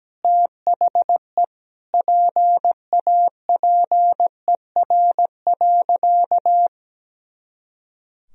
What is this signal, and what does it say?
Morse code, heard as a keyed tone, about 17 wpm, 700 Hz. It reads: THE PAPER.
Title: Sentences from Top 300 Words and Abbreviations 17wpm